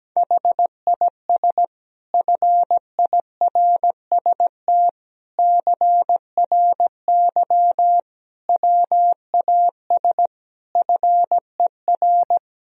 Morse code, heard as a keyed tone, about 17 wpm, 700 Hz. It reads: HIS FIRST CRY WAS FER
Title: Sentences from Top 500 Words and Abbreviations 17wpm